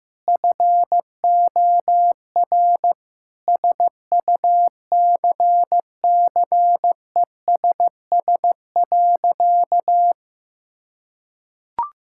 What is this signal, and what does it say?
Morse code, heard as a keyed tone, about 15 wpm, 700 Hz. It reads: FOR SUCCESS.
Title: Sentences from Top 1000 Words - Review Part 5 - 15wpm